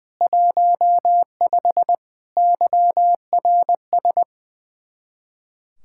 Morse code, heard as a keyed tone, about 20 wpm, 700 Hz. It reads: 15 YRS
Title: Speed-Racing Common QSO Phrases - 20wpm